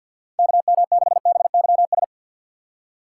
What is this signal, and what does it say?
Morse code, heard as a keyed tone, about 53 wpm, 700 Hz. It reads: KM6BYS
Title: USA Call Signs - 6 Characters - 4x Character Spacing - 50wpm